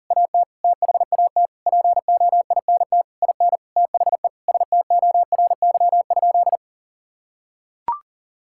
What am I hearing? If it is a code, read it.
Morse code, heard as a keyed tone, about 40 wpm, 700 Hz. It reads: AT THAT POINT IN THE STORY?
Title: Sentences from Top 1000 Words - Review Part 5 - 40wpm